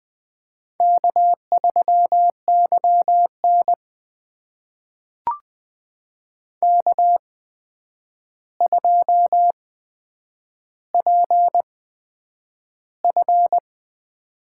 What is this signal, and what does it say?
Morse code, heard as a keyed tone, about 20 wpm, spaced as for 9 wpm, 700 Hz. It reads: K3YN K2PF
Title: USA Call Signs - 4 Characters - 8x Character Spacing - 20wpm